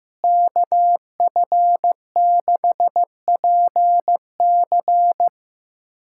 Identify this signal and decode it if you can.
Morse code, heard as a keyed tone, about 15 wpm, 700 Hz. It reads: KF6PC
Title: USA Callsign DE Callsign - 5 Characters - 15wpm